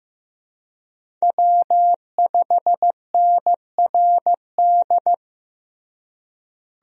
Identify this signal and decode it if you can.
Morse code, heard as a keyed tone, about 15 wpm, 700 Hz. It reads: W5NRD
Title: USA Call Signs - 5 Characters - Repeated 2x - 15wpm